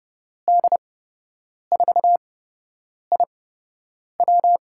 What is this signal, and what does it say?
Morse code, heard as a keyed tone, about 30 wpm, spaced as for 8 wpm, 700 Hz. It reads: D4IW
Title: USA Call Signs - 5 Characters - 8x Character Spacing - 30wpm